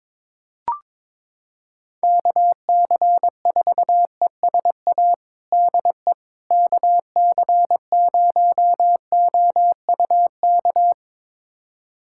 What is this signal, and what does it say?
Morse code, heard as a keyed tone, about 22 wpm, 700 Hz. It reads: KC4ESA DE KC0OUK
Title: USA Callsign DE Callsign - 6 Characters - 22wpm